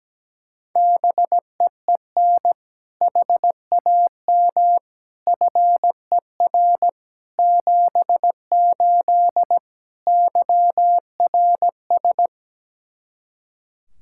Morse code, heard as a keyed tone, about 17 wpm, 700 Hz. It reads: BEEN HAM FER 78 YRS
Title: Practice QSO - Exchange 3 - More Info - 17wpm